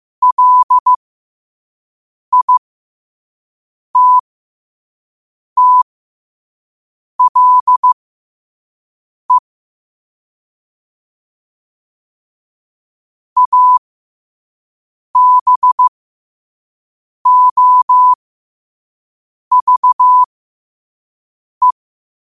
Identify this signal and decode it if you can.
Morse code, heard as a keyed tone, about 15 wpm, spaced as for 5 wpm, 1000 Hz. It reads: LITTLE ABOVE
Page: New quotes every day in morse code at 5 Words per minute.